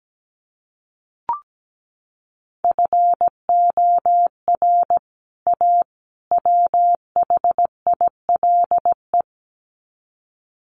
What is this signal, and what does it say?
Morse code, heard as a keyed tone, about 17 wpm, 700 Hz. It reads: FOR A WHILE
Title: Sets of 3 Words 17wpm